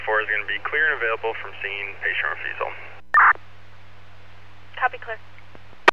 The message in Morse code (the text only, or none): none